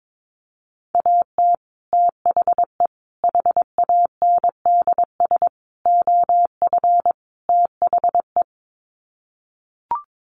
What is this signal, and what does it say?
Morse code, heard as a keyed tone, about 22 wpm, 700 Hz. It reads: AT THE HANDS OF THE
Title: Sets of 5 Words Encore2 22wpm